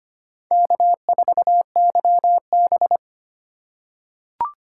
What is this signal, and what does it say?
Morse code, heard as a keyed tone, about 25 wpm, 700 Hz. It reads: K4YB
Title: USA Call Signs - 4 Characters - 6x Character Spacing - 25wpm